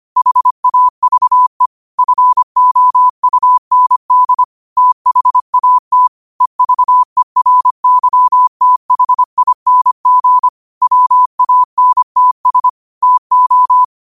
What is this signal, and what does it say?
Morse code, heard as a keyed tone, about 25 wpm, 1000 Hz. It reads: HAVE FOUND THAT EVERYTHING WANTS TO
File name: qotdfiles / 25wpm